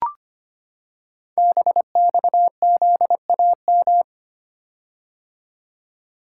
Morse code, heard as a keyed tone, about 25 wpm, 700 Hz. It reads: BXZAM
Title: Code Groups - 5 Letters - 25wpm